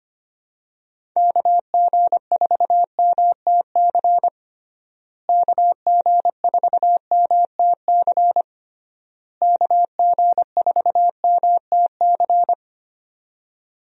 Morse code, heard as a keyed tone, about 25 wpm, 700 Hz. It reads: KG4MTC KG4MTC KG4MTC
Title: USA Call Signs - 6 Characters - Repeated 3x - 25wpm